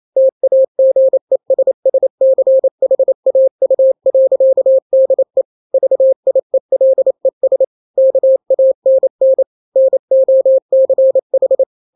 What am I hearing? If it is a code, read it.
Morse code, heard as a keyed tone, about 28 wpm, 530 Hz. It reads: TAGESSCHAU.DE VIELES KANN NOCH
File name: daily / 27wpm